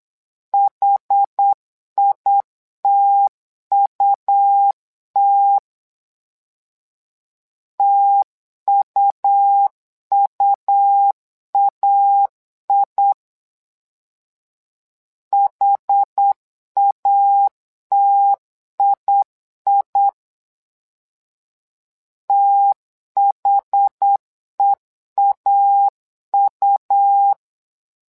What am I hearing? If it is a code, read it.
Morse code, heard as a keyed tone, about 8 wpm, 800 Hz. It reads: HITUT TUUAI HATII THEAU
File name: LEZIONI / 2 UHDN / 2 UH 8wpm 2129